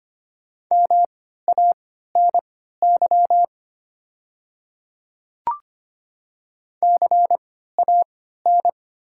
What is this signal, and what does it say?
Morse code, heard as a keyed tone, about 25 wpm, spaced as for 14 wpm, 700 Hz. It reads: MANY CAN
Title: 3x Character Spacing - Top 100 Words - 25wpm